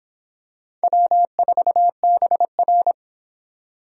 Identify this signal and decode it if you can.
Morse code, heard as a keyed tone, about 25 wpm, 700 Hz. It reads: W4BR